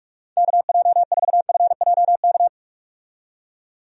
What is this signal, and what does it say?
Morse code, heard as a keyed tone, about 44 wpm, 700 Hz. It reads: KJ4FJK
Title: USA Call Signs - 6 Characters - 45wpm